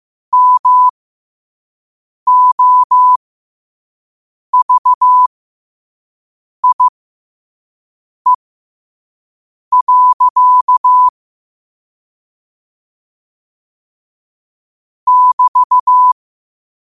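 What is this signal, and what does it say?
Morse code, heard as a keyed tone, about 15 wpm, spaced as for 5 wpm, 1000 Hz. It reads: MOVIE. <BT>
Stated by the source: Morse code quote-of-the-day practice at 5 wpm.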